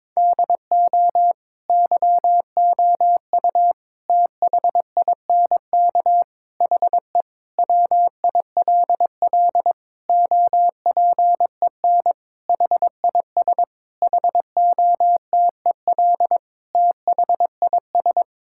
Morse code, heard as a keyed tone, about 22 wpm, 700 Hz. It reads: DO YOU THINK HE WILL OPEN HIS HOTEL THIS
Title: Sentences from Top 1600 Words 22wpm